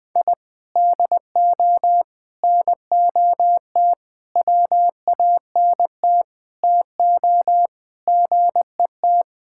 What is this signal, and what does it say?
Morse code, heard as a keyed tone, about 20 wpm, 700 Hz. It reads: I DO NOT WANT TO GET
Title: Sentences with 5-Word Phrases - 20wpm